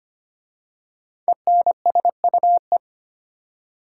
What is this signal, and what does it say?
Morse code, heard as a keyed tone, about 25 wpm, 700 Hz. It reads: ENSUE
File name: 5 Letter Words - First 3 Letters Given - 25wpm